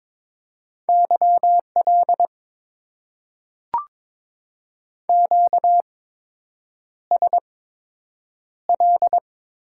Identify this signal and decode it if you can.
Morse code, heard as a keyed tone, about 21 wpm, spaced as for 8 wpm, 700 Hz. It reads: YL QSL
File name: Top 90 QSO Elements - 8x Character Spacing - 22wpm